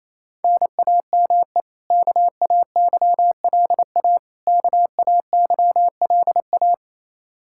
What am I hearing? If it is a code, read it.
Morse code, heard as a keyed tone, about 28 wpm, 700 Hz. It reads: NAME KAYLA KAYLA
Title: Practice QSO - Exchange 1 - Beginning - 28wpm